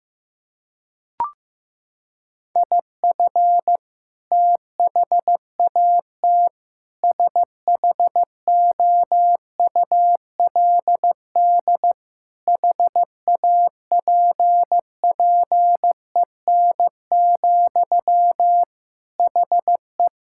Morse code, heard as a keyed tone, about 15 wpm, 700 Hz. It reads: IF THAT SHOULD HAPPEN, HE
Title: Sentences from Top 1000 Words - Review Part 6 - 15wpm